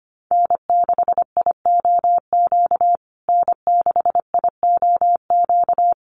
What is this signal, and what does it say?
Morse code, heard as a keyed tone, about 25 wpm, 700 Hz. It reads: N6IOQ N6IOQ
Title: USA Call Signs Repeated 3x 25wpm